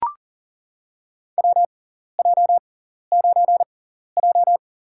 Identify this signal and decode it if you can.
Morse code, heard as a keyed tone, about 40 wpm, spaced as for 14 wpm, 700 Hz. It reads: WJ9J
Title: USA Call Signs - 4 Characters - 6x Character Spacing - 40wpm